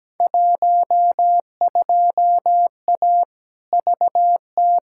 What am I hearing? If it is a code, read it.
Morse code, heard as a keyed tone, about 17 wpm, 700 Hz. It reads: 12A VT
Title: Field Day Exchanges Encore - 17wpm